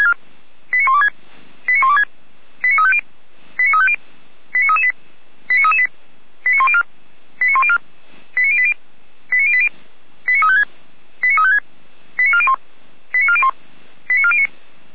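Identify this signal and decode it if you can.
Five-tone selective call (ZVEI), heard as quick runs of five tones, tones of about 80 ms. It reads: 89126 89126 89350 89350 89338 89338 89104 89104 89080 89080 89356 89356 89401 89401 89409
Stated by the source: ZVEI five-tone selective call received by SDR, from the Feuerwehr Orlamünde audio archive.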